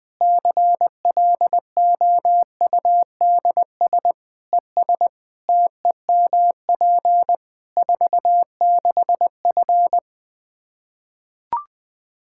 Morse code, heard as a keyed tone, about 20 wpm, 700 Hz. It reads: CLOUDS ES TEMP 46F
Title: Practice QSO - Exchange 2 - Info - 20wpm